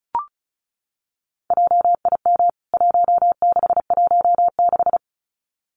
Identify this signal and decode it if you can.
Morse code, heard as a keyed tone, about 34 wpm, 700 Hz. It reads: JIM 1616
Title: CWT Exchange 35wpm